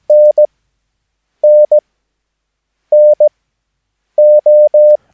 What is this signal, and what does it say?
Morse code, heard as a keyed tone, about 17 wpm, spaced as for 7 wpm, 600 Hz. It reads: NNNO
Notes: Övningsmaterial med alfabetet A till Ö samt siffror 0 till 9 enligt International/Continental Morse Code.